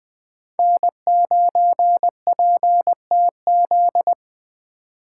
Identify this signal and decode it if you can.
Morse code, heard as a keyed tone, about 24 wpm, 700 Hz. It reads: N9PTZ
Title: Speed Racing - USA 1x3 Call Signs 20wpm